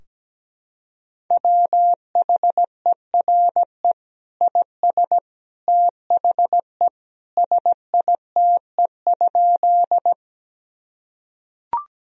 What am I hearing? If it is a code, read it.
Morse code, heard as a keyed tone, about 17 wpm, 700 Hz. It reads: WHERE IS THE SITE?
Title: Sentences from Top 1000 Words - Review Part 4 - 17wpm